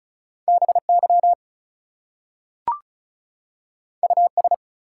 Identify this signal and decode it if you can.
Morse code, heard as a keyed tone, about 36 wpm, 700 Hz. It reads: BY US
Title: Two Letter Words - 35wpm